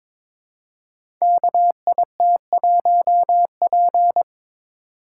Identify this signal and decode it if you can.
Morse code, heard as a keyed tone, about 22 wpm, 700 Hz. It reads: KIT1P
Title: Code Groups - 5 Letter:Numbers - 22wpm